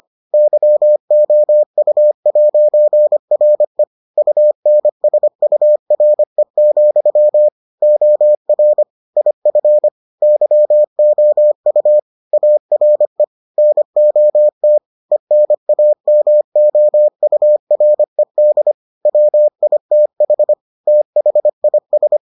Morse code, heard as a keyed tone, about 25 wpm, 600 Hz. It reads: YOU'RE UNSURE, OR IF YOU ARE NOT ENAMOURED WITH THIS